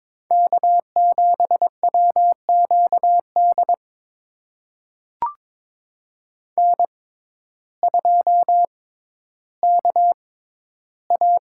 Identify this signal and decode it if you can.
Morse code, heard as a keyed tone, about 22 wpm, spaced as for 12 wpm, 700 Hz. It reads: K7WQD N2KA
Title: USA Call Signs - 5 Characters - 6x Character Spacing - 22wpm